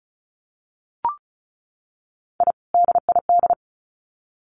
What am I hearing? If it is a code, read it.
Morse code, heard as a keyed tone, about 34 wpm, 700 Hz. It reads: I DID
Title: Sets of 2 Words 35wpm